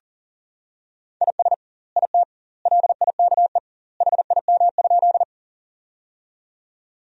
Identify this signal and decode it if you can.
Morse code, heard as a keyed tone, about 40 wpm, 700 Hz. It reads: IS IT LIKE HIM?
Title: 2x Word-Spacing - Sentences from Top 100 Words - 40wpm